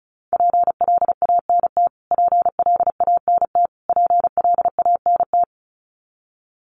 Morse code, heard as a keyed tone, about 34 wpm, 700 Hz. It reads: PLANT PLANT PLANT
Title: Top 500 Words Repeated 35wpm